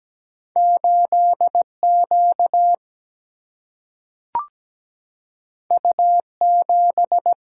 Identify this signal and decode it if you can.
Morse code, heard as a keyed tone, about 17 wpm, 700 Hz. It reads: 8 Q U 7
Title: Single-Letter Single-Number 17wpm